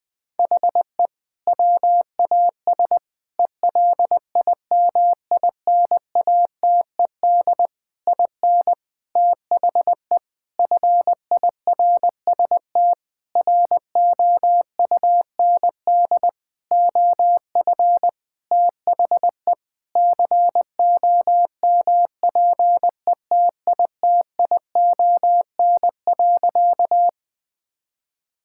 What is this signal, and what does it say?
Morse code, heard as a keyed tone, about 20 wpm, 700 Hz. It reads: HE WAS ELIMINATED IN THE FIRST ROUND OF THE COMPETITION.
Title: Sentences with 6-Word Phrases - 20wpm